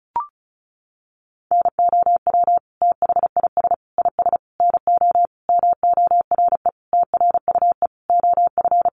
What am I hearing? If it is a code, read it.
Morse code, heard as a keyed tone, about 34 wpm, 700 Hz. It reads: NOW THIS IS NO MORE TRUE OF
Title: Sentences from Top 1400 Words 35wpm